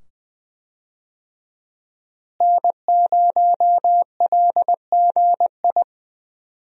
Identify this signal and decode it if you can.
Morse code, heard as a keyed tone, about 20 wpm, 700 Hz. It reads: N0LGI
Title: USA Call Signs - 5 Characters - Repeated 2x - 20wpm